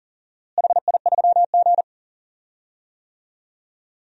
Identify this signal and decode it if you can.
Morse code, heard as a keyed tone, about 40 wpm, 700 Hz. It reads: HI3G
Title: DXCC-100 Call Signs - 4 Characters - Repeated 2x - 40wpm